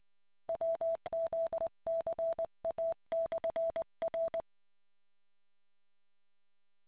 Morse code, heard as a keyed tone, about 20 wpm, 660 Hz. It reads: WZCA/R